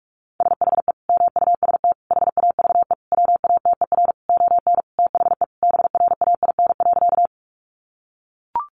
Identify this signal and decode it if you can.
Morse code, heard as a keyed tone, about 44 wpm, 700 Hz. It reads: SHE MUST HAVE WATER ON THE BRAIN.
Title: Sentences from Top 1100 Words 45wpm